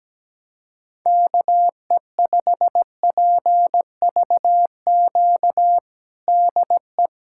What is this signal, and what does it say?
Morse code, heard as a keyed tone, about 17 wpm, 700 Hz. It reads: KE5PVQ DE
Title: USA Callsign DE Callsign 17wpm